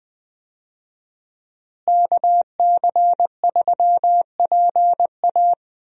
Morse code, heard as a keyed tone, about 21 wpm, 700 Hz. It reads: KC3PA
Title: USA Call Signs - 5 Characters - Repeated 3x - 20wpm